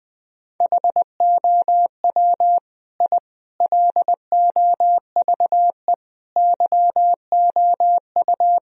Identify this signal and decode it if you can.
Morse code, heard as a keyed tone, about 20 wpm, 700 Hz. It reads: HOW I LOVE YOU
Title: Sentences from Top 1000 Words - Review Part 1 - 20wpm